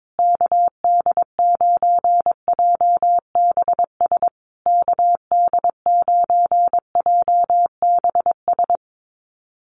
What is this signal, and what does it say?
Morse code, heard as a keyed tone, about 22 wpm, 700 Hz. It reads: KD9JBS KD9JBS
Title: USA Call Signs Repeated 3x 22wpm